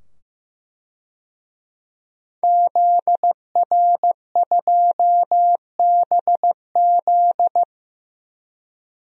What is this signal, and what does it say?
Morse code, heard as a keyed tone, about 15 wpm, 700 Hz. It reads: ZR2BZ
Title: DXCC-100 Call Signs - 5 Characters - 15wpm